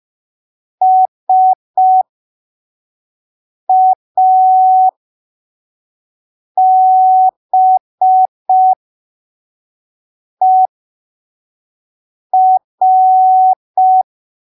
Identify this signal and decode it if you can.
Morse code, heard as a keyed tone, about 5 wpm, spaced as for 3 wpm, 750 Hz. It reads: SABER
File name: arquivos / texto / texto 15 05ppm